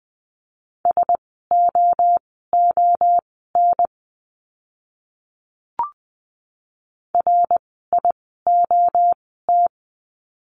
Morse code, heard as a keyed tone, about 20 wpm, spaced as for 15 wpm, 700 Hz. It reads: SOON RIOT
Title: 054 Lesson 10f - Letter r - Words 20wpm